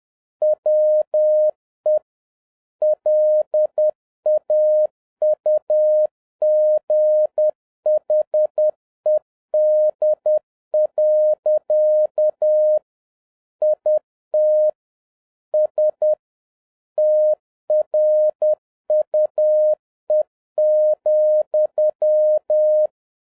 Morse code, heard as a keyed tone, about 10 wpm, 610 Hz. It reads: WE LAUGHED. IT S TRUE,
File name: 2011-07-mp3 / n46 JOHN K3TN 10wpm